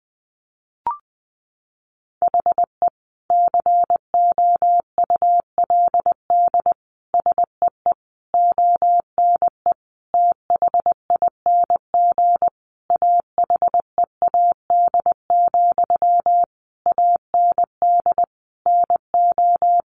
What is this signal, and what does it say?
Morse code, heard as a keyed tone, about 20 wpm, 700 Hz. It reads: HE COULD SEE ONE THING AHEAD, AND NO
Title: Sentences from Top 1100 Words 20wpm